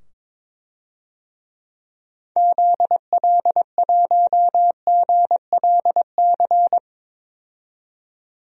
Morse code, heard as a keyed tone, about 21 wpm, 700 Hz. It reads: ZL1GLC
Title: DXCC-100 Call Signs - 6 Characters - 22wpm